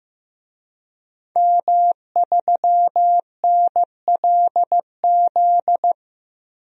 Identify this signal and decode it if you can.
Morse code, heard as a keyed tone, about 15 wpm, 700 Hz. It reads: M3NLZ
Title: DXCC-100 Call Signs - 5 Characters - 15wpm